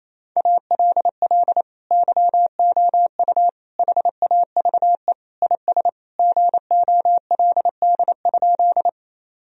Morse code, heard as a keyed tone, about 28 wpm, 700 Hz. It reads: ALL YOU HAVE IS GOLD?
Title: Sentences from Top 1000 Words - Review Part 3 - 28wpm